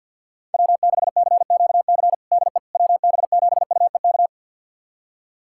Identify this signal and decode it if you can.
Morse code, heard as a keyed tone, about 53 wpm, 700 Hz. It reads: W6CQC DE WB7UEK
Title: USA Callsign DE Callsign 50wpm